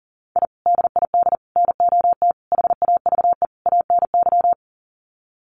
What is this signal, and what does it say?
Morse code, heard as a keyed tone, about 40 wpm, 700 Hz. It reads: I DID NOT HAVE ANY
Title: Sets of 5 Words Encore 40wpm